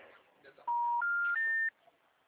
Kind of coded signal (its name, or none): telephone call progress tone